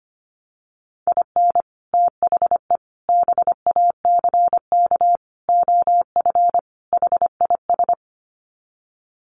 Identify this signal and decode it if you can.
Morse code, heard as a keyed tone, about 25 wpm, 700 Hz. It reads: IN THE BACK OF HIS
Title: Sets of 5 Words Encore2 25wpm